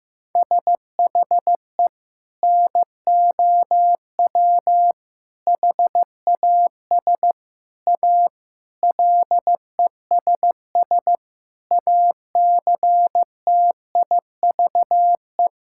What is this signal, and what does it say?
Morse code, heard as a keyed tone, about 15 wpm, 700 Hz. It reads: SHE NOW HAS A LESS ACTIVE
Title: Sentences from Top 1000 Words - Review Part 6 - 15wpm